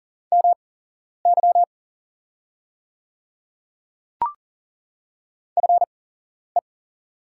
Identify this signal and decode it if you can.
Morse code, heard as a keyed tone, about 39 wpm, spaced as for 11 wpm, 690 Hz. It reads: MY FE
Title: 8x Character Spacing - Top 500 Words - 40wpm